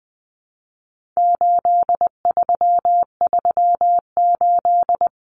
Morse code, heard as a keyed tone, about 20 wpm, 700 Hz. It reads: 8338
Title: Numbers - 4-Digit 20wpm